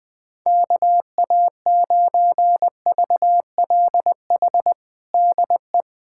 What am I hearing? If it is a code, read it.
Morse code, heard as a keyed tone, about 21 wpm, 700 Hz. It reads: KA9VLH DE
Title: USA Callsign DE Callsign - 6 Characters - 20wpm